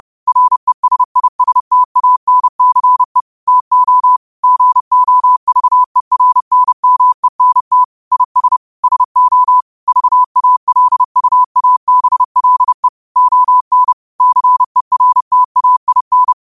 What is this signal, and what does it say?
Morse code, heard as a keyed tone, about 30 wpm, 1000 Hz. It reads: RESISTANCE TO GOVERNMENT IS SO VALUABLE ON CERTAIN
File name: qotdfiles / 30wpm